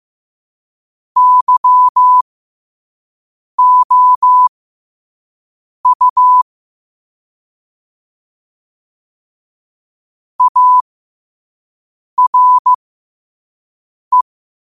Morse code, heard as a keyed tone, about 15 wpm, spaced as for 5 wpm, 1000 Hz. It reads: YOU ARE
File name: qotdfiles / 5wpm